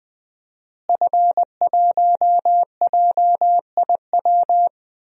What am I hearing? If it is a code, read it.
Morse code, heard as a keyed tone, about 20 wpm, 700 Hz. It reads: F1JIW
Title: DXCC-100 Call Signs - 5 Characters - Repeated 3x - 20wpm